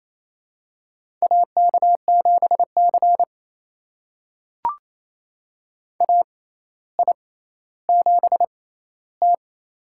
Morse code, heard as a keyed tone, about 28 wpm, spaced as for 14 wpm, 700 Hz. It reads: AK7C AI7T
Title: USA Call Signs - 4 Characters - 6x Character Spacing - 28wpm